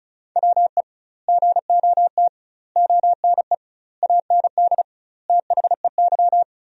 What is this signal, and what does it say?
Morse code, heard as a keyed tone, about 34 wpm, 700 Hz. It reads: WE GOT ONE AND THEY
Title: 2x Word-Spacing - Sentences from Top 300 Words - 35wpm